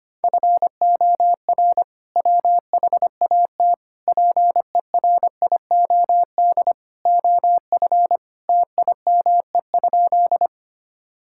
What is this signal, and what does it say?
Morse code, heard as a keyed tone, about 25 wpm, 700 Hz. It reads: FOR WHAT PERIOD OF TIME?
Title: Sentences from Top 1000 Words - Review Part 5 - 25wpm